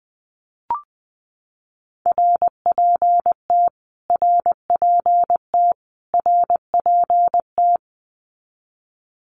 Morse code, heard as a keyed tone, about 20 wpm, 700 Hz. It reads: RPT RPT RPT
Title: Abbreviations Repeated 3x - 20wpm